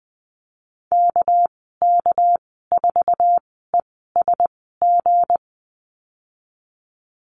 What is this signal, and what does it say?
Morse code, heard as a keyed tone, about 20 wpm, spaced as for 15 wpm, 700 Hz. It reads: KK4ESG
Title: 200 Lesson 32g - Letter k - Call Signs 20wpm